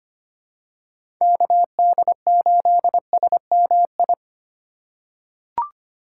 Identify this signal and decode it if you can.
Morse code, heard as a keyed tone, about 25 wpm, 700 Hz. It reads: KD8SMI
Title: USA Call Signs - 3x Character Spacing - 25wpm